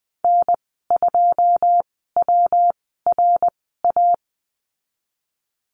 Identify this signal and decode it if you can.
Morse code, heard as a keyed tone, about 20 wpm, spaced as for 15 wpm, 700 Hz. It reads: N2WRA
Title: 122 Lesson 20g - Number 3 - Call Signs 20wpm